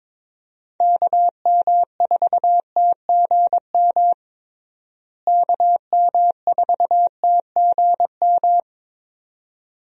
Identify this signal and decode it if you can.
Morse code, heard as a keyed tone, about 22 wpm, 700 Hz. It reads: KM4TGM KM4TGM
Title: USA Call Signs - 6 Characters - Repeated 2x - 22wpm